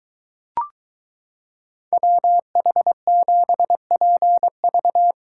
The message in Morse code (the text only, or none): WH7PV